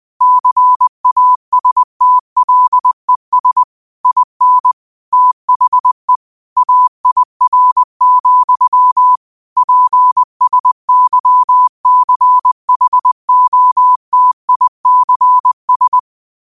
Morse code, heard as a keyed tone, about 20 wpm, 1000 Hz. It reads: CASTLES IN THE AIR, PSYCHOTICS
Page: New quotes every day in morse code at 20 Words per minute.